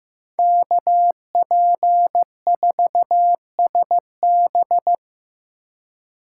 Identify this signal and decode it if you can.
Morse code, heard as a keyed tone, about 18 wpm, 700 Hz. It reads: KP4SB
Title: Speed Racing - USA 2x2 Call Signs 15wpm